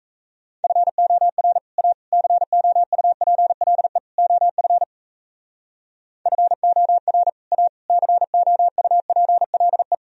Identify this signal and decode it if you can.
Morse code, heard as a keyed tone, about 35 wpm, 700 Hz. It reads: FOR A COUPLE OF FOR A COUPLE